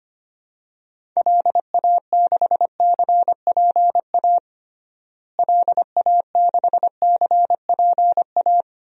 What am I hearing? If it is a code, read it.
Morse code, heard as a keyed tone, about 25 wpm, 700 Hz. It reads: LA6CPA LA6CPA
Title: DXCC-100 Call Signs - 6 Characters - Repeated 2x - 25wpm